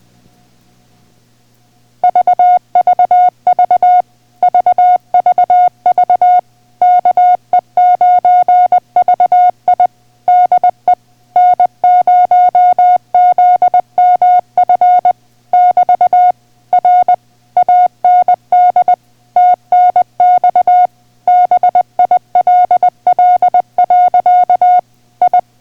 Morse code, heard as a keyed tone, about 20 wpm, 720 Hz. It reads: VVV VVV KE9VI DE N0ZMF <BT> R AND TNX BILL. I